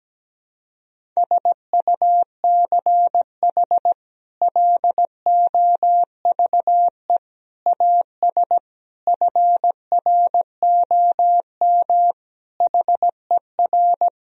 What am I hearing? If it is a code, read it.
Morse code, heard as a keyed tone, about 17 wpm, 700 Hz. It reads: SUCH LOVE AS FROM HER
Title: Sentences from Top 1000 Words - Review Part 1 - 17wpm